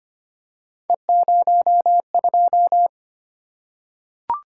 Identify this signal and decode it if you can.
Morse code, heard as a keyed tone, about 25 wpm, 700 Hz. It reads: E02
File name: Three Letter-Number Combinations - 25wpm